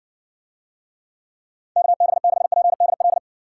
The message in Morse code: KB6PDL

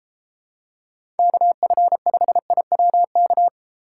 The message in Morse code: KF5IWK

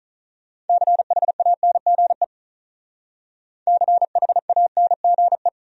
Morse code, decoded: CHANGE CHANGE

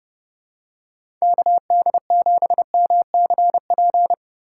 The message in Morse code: KD7MCP

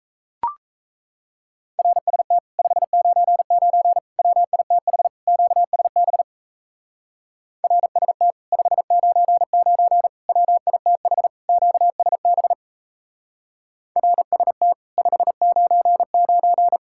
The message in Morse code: RST 599 WITH QSB RST 599 WITH QSB RST 599

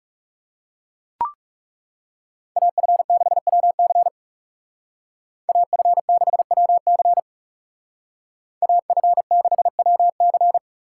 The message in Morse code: AF6WC AF6WC AF6WC

